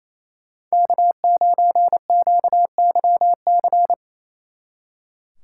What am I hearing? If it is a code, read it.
Morse code, heard as a keyed tone, about 35 wpm, 700 Hz. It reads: K9QYC